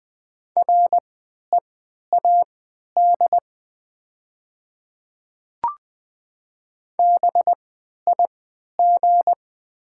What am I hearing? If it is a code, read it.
Morse code, heard as a keyed tone, about 20 wpm, spaced as for 11 wpm, 700 Hz. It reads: READ BIG
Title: 3x Character Spacing - Top 300 Words - 20wpm